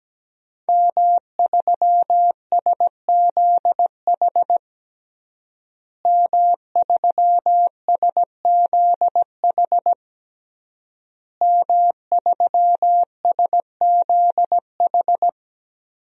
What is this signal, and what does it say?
Morse code, heard as a keyed tone, about 17 wpm, 700 Hz. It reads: M3SZH M3SZH M3SZH
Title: DXCC-100 Call Signs - 5 Characters - Repeated 3x - 17wpm